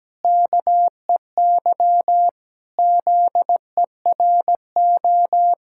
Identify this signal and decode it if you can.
Morse code, heard as a keyed tone, about 17 wpm, 700 Hz. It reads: KEY ZERO
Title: Pangram Challenge v2 - 17wpm